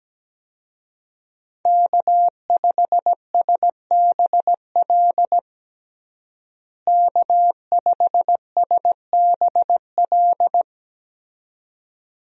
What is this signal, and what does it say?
Morse code, heard as a keyed tone, about 17 wpm, 700 Hz. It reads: K5SBL K5SBL
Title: USA Call Signs - 5 Characters - Repeated 2x - 17wpm